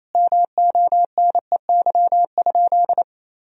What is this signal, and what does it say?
Morse code, heard as a keyed tone, about 28 wpm, 700 Hz. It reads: MONEY?
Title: Sentences from Top 400 Words and Abbreviations 28wpm